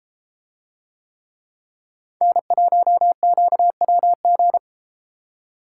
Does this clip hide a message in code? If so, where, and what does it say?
Yes: Morse code from 2.2 to 4.6 s, heard as a keyed tone, about 32 wpm, 700 Hz. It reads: N1QWG